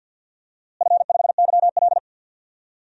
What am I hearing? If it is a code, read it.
Morse code, heard as a keyed tone, about 51 wpm, 700 Hz. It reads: F5YL
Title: DXCC-100 Call Signs - 4 Characters - Repeated 2x - 50wpm